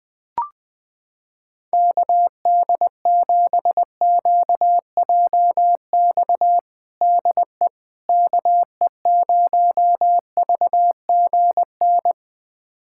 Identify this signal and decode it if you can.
Morse code, heard as a keyed tone, about 20 wpm, 700 Hz. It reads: KD7QJX DE KE0VGN
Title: USA Callsign DE Callsign 20wpm